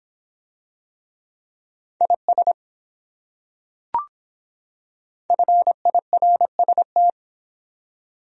Morse code, heard as a keyed tone, about 24 wpm, 700 Hz. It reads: IS FIRST